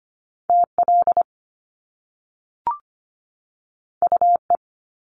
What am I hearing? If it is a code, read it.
Morse code, heard as a keyed tone, about 25 wpm, 700 Hz. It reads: TL UE
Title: Top Two Letter Combinations in Words 25wpm